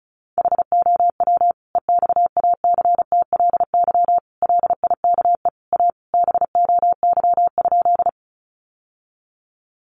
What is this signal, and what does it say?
Morse code, heard as a keyed tone, about 34 wpm, 700 Hz. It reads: HOW EXACTLY LIKE A BOY?
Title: Sentences from Top 900 Words 35wpm